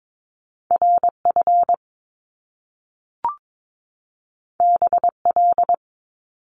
Morse code, heard as a keyed tone, about 22 wpm, 700 Hz. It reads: RF BL
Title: Top Two Letter Combinations in Words 22wpm